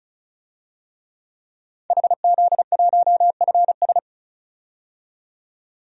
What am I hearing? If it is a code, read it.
Morse code, heard as a keyed tone, about 34 wpm, 700 Hz. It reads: HZ1FS
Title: DXCC-100 Call Signs - 5 Characters - 35wpm